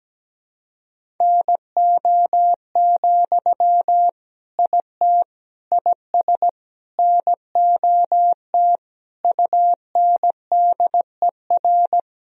Morse code, heard as a keyed tone, about 17 wpm, 700 Hz. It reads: NO, IT IS NOT UNDER
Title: Sentences from Top 1000 Words - Review Part 1 - 17wpm